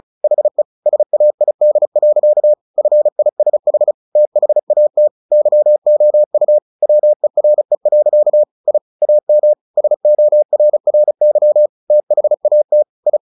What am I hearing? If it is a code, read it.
Morse code, heard as a keyed tone, about 34 wpm, 600 Hz. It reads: HE SAID. FISH THAT YOU WERE. I AM SORRY THAT I
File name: the old man and the sea 2 35wpm